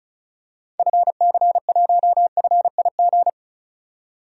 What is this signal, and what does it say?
Morse code, heard as a keyed tone, about 36 wpm, 700 Hz. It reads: FC1FIG
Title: DXCC-100 Call Signs - 6 Characters - 35wpm